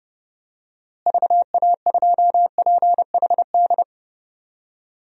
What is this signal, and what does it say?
Morse code, heard as a keyed tone, about 29 wpm, 700 Hz. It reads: VA2PHD
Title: DXCC-100 Call Signs - 6 Characters - Repeated 2x - 30wpm